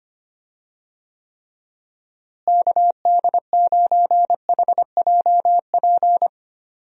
Morse code, heard as a keyed tone, about 25 wpm, 700 Hz. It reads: KD9HJP
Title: USA Call Signs - 6 Characters - Repeated 2x - 25wpm